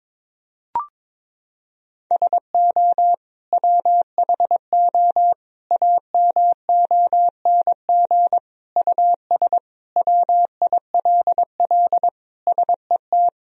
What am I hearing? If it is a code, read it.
Morse code, heard as a keyed tone, about 22 wpm, 700 Hz. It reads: SO WHO AMONG US WILL SET
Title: Sentences from Top 1000 Words - Review Part 3 - 22wpm